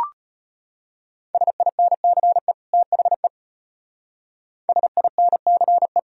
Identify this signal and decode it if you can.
Morse code, heard as a keyed tone, about 33 wpm, 700 Hz. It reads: SINCE THE SINCE